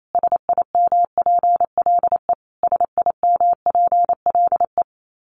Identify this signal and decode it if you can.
Morse code, heard as a keyed tone, about 28 wpm, 700 Hz. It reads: SIMPLE SIMPLE
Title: Top 500 Words Repeated 28wpm